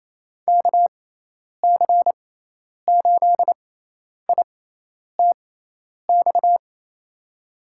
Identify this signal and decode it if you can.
Morse code, heard as a keyed tone, about 28 wpm, spaced as for 10 wpm, 700 Hz. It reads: KC8ITX
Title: USA Call Signs - 6 Characters - 6x Character Spacing - 28wpm